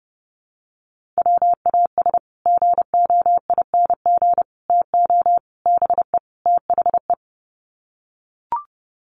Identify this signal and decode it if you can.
Morse code, heard as a keyed tone, about 30 wpm, 700 Hz. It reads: WAS GOING TO BE THE
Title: Sets of 5 Words Encore2 30wpm